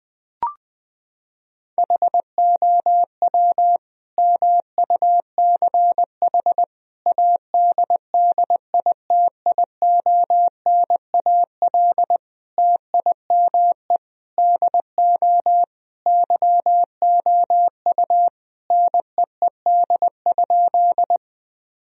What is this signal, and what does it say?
Morse code, heard as a keyed tone, about 20 wpm, 700 Hz. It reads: HOW MUCH ADDITIONAL TIME DO YOU NEED?
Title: Sentences from Top 1000 Words - Review Part 4 - 20wpm